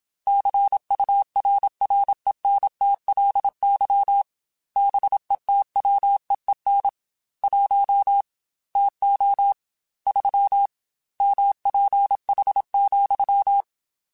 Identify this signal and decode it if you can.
Morse code, heard as a keyed tone, about 27 wpm, 800 Hz. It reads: CURRENTLY BETWEEN 1 TO 3 MPH,
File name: Weather Sets / Weather Sets-mp3s / Weather 5 5 @25